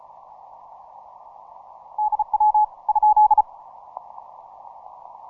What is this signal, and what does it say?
Morse code, heard as a keyed tone, about 35 wpm, 850 Hz. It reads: DW ?
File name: simple-file-list / tx5k 15mcw